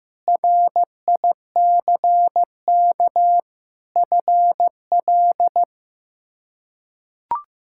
Morse code, heard as a keyed tone, about 15 wpm, 700 Hz. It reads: RICK FL
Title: K1USN SST QSO Practice - 15wpm